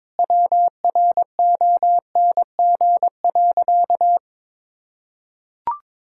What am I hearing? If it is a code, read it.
Morse code, heard as a keyed tone, about 22 wpm, 700 Hz. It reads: WRONG.
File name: Sentences with 4-Word Phrases - 22wpm